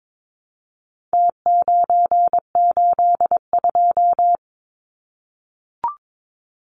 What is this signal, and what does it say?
Morse code, heard as a keyed tone, about 22 wpm, 700 Hz. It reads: T982
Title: Numbers - Contest Style - 4-Digit 22wpm